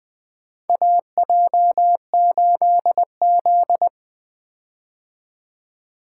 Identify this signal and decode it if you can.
Morse code, heard as a keyed tone, about 24 wpm, 700 Hz. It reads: AJ8Z
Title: Speed-Racing - Four Letter-Number Combinations - 20wpm